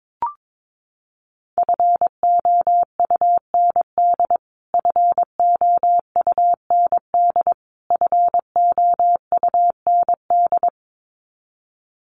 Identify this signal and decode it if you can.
Morse code, heard as a keyed tone, about 22 wpm, 700 Hz. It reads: FOUND FOUND FOUND
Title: Top 500 Words Repeated 22wpm